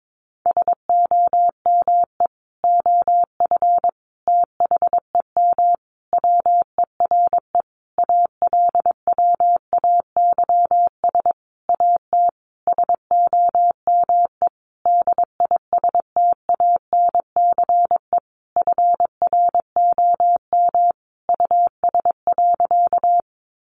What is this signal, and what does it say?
Morse code, heard as a keyed tone, about 22 wpm, 700 Hz. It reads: SOME OF THEM WERE ALWAYS AT SOME DISTANCE FROM US.
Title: Sentences from Top 1200 Words 22wpm